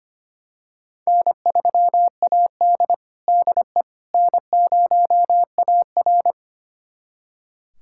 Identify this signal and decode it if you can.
Morse code, heard as a keyed tone, about 25 wpm, 700 Hz. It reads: N3AD DE N0AR
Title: USA Callsign DE Callsign - 4 Characters - 25wpm